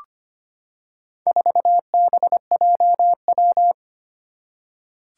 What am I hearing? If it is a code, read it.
Morse code, heard as a keyed tone, about 25 wpm, 700 Hz. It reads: 4BJW
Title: Four Challenging Letter-Number Combinations 25wpm